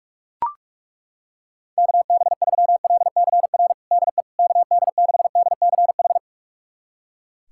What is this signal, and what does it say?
Morse code, heard as a keyed tone, about 41 wpm, 700 Hz. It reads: KB3LCR DE KD6DCH
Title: USA Callsign DE Callsign - 6 Characters - 45wpm